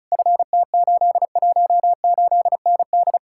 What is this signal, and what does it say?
Morse code, heard as a keyed tone, about 48 wpm, spaced as for 42 wpm, 700 Hz. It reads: FT818ND
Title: Speed-Racing Common QSO Phrases - 35wpm